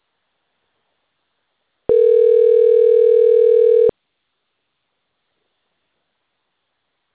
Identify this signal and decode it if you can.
Ringback tone (North America), 440 + 480 Hz.